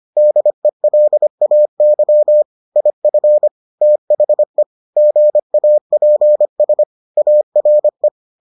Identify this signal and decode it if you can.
Morse code, heard as a keyed tone, about 25 wpm, 600 Hz. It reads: DELAY IF THE GAPS ARE